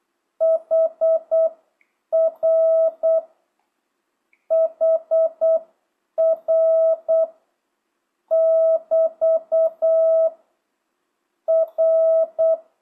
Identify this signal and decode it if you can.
Morse code, heard as a keyed tone, about 8 wpm, spaced as for 7 wpm, 650 Hz. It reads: HR HR <BT> R